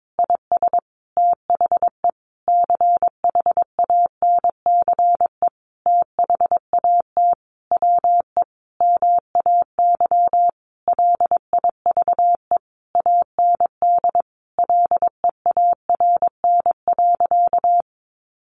Morse code, heard as a keyed tone, about 22 wpm, 700 Hz. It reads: IS THE CHANCE THAT WE MAY LIVE AND LEARN.
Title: Sentences from Top 800 Words 22wpm